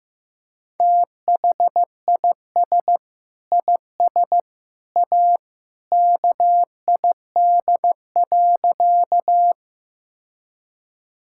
Morse code, heard as a keyed tone, about 15 wpm, 700 Hz. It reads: THIS IS A KID.
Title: Sentences from Top 1600 Words 15wpm